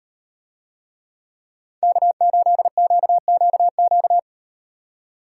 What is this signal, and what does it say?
Morse code, heard as a keyed tone, about 35 wpm, 700 Hz. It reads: K8QQQ